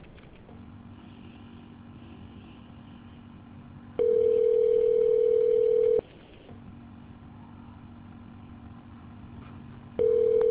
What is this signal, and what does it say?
Ringback tone (North America), 440 + 480 Hz.